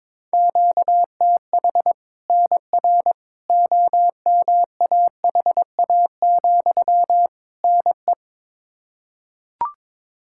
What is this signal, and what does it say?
Morse code, heard as a keyed tone, about 22 wpm, 700 Hz. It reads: QTH NR OMAHA, NE
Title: Practice QSO - Exchange 1 - Beginning - 22wpm